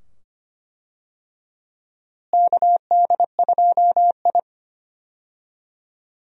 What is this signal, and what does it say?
Morse code, heard as a keyed tone, about 25 wpm, 700 Hz. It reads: KD2I
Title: USA Call Signs - 4 Characters - Repeated 2x - 25wpm